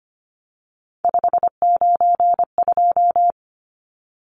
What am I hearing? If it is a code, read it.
Morse code, heard as a keyed tone, about 25 wpm, 700 Hz. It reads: 592
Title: Numbers - 3-Digit 25wpm